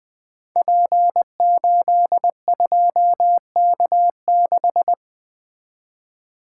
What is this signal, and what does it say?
Morse code, heard as a keyed tone, about 20 wpm, 700 Hz. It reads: P82K6
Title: Code Groups - 5 Letter:Numbers - 20wpm